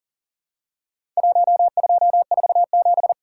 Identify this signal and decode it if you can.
Morse code, heard as a keyed tone, about 40 wpm, 700 Hz. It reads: 1247
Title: Worked All Europe DX Contest - QTC Exchange - 40wpm